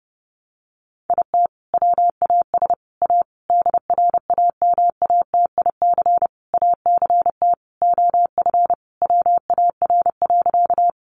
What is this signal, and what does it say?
Morse code, heard as a keyed tone, about 30 wpm, 700 Hz. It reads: IT WAS A DRAMATIC ACT OF WAR.
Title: Sentences from Top 1300 Words 30wpm